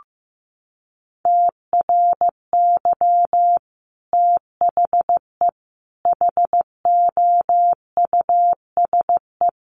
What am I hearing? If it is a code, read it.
Morse code, heard as a keyed tone, about 14 wpm, 700 Hz. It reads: TRY THE HOUSE
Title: Sentences from Top 1300 Words 15wpm